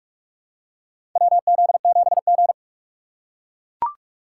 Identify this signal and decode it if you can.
Morse code, heard as a keyed tone, about 43 wpm, 700 Hz. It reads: WZ7G
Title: USA Call Signs - 3x Character Spacing - 45wpm